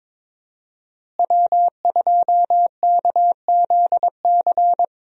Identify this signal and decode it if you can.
Morse code, heard as a keyed tone, about 22 wpm, 700 Hz. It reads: W2KZC
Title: USA Call Signs - 5 Characters - Repeated 3x - 22wpm